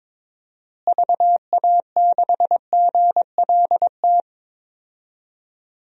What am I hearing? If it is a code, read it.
Morse code, heard as a keyed tone, about 22 wpm, 700 Hz. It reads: VA6GLT
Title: DXCC-100 Call Signs - 6 Characters - Repeated 2x - 22wpm